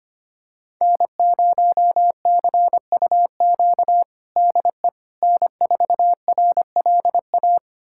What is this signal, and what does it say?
Morse code, heard as a keyed tone, about 25 wpm, 700 Hz. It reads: N0CUQ DE N4RLA
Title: USA Callsign DE Callsign - 5 Characters - 25wpm